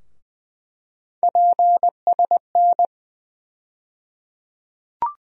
Morse code, heard as a keyed tone, about 21 wpm, 700 Hz. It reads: PSN
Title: USA Call Sign Postfixes - 3 Letters 20wpm